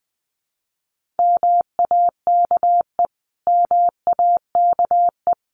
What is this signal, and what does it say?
Morse code, heard as a keyed tone, about 20 wpm, 700 Hz. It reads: MAKE MAKE
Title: Top 200 Words Repeated 20wpm